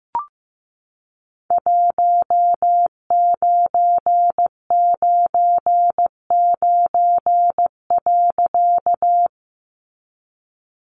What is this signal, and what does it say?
Morse code, heard as a keyed tone, about 15 wpm, 700 Hz. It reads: 1999.
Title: A Sound Of Thunder 15wpm